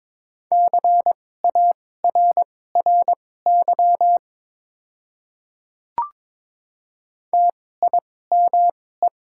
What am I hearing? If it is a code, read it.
Morse code, heard as a keyed tone, about 22 wpm, spaced as for 16 wpm, 690 Hz. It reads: CARRY TIME
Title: 2x Character Spacing - Top 500 Words - 22wpm